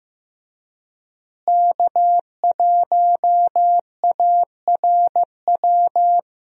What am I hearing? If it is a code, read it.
Morse code, heard as a keyed tone, about 15 wpm, 700 Hz. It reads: K1ARW
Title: USA Call Signs - 5 Characters - Repeated 3x - 15wpm